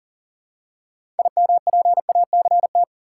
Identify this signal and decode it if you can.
Morse code, heard as a keyed tone, about 40 wpm, 700 Hz. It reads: IMPACT
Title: Six Letter Words - 40wpm